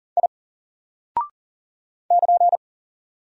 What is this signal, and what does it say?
Morse code, heard as a keyed tone, about 40 wpm, 700 Hz. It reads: I <KN>
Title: Single Character (Letter:Number:Punctuation:Prosign) Alt 40wpm